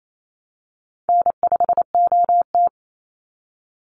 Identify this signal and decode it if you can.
Morse code, heard as a keyed tone, about 28 wpm, 700 Hz. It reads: N5OT
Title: Easy Call Signs Repeated 2x - 28wpm